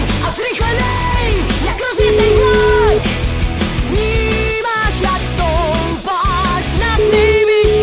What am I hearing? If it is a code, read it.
Ringback tone (Europe, 425 Hz), with noise.